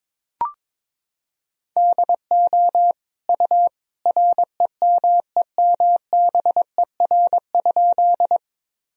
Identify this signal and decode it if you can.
Morse code, heard as a keyed tone, about 21 wpm, 700 Hz. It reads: DO U REMEMBER?
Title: Sentences from Top 400 Words and Abbreviations 22wpm